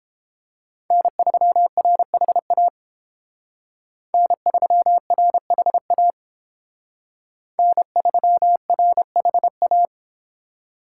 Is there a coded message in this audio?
Yes: Morse code from 0.9 to 9.9 s, heard as a keyed tone, about 29 wpm, 700 Hz. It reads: N3RHA N3RHA N3RHA